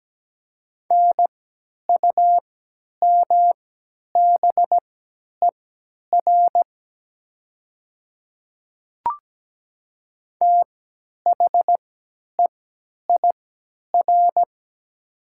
Morse code, heard as a keyed tone, about 17 wpm, spaced as for 10 wpm, 700 Hz. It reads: NUMBER THEIR
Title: 3x Character Spacing - Top 100 Words - 17wpm